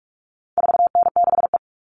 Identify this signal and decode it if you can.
Morse code, heard as a keyed tone, about 44 wpm, 700 Hz. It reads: 4N6E
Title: All Cut Numbers - 4-Digit 45wpm